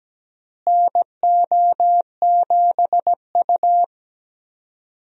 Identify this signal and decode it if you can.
Morse code, heard as a keyed tone, about 17 wpm, 700 Hz. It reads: NO7U
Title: USA Call Signs - 4 Characters - Repeated 2x - 17wpm